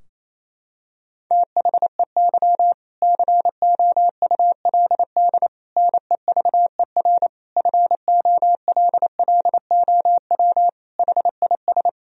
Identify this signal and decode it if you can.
Morse code, heard as a keyed tone, about 28 wpm, 700 Hz. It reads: THEY COULD NEVER FOLLOW HIS
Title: Sentences from Top 1000 Words - Review Part 4 - 28wpm